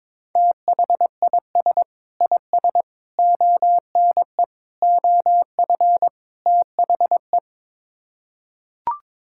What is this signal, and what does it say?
Morse code, heard as a keyed tone, about 22 wpm, 700 Hz. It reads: THIS IS ONE OF THE
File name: practice / 22wpm / Build A Phrase - 5-Word Phrases - 22wpm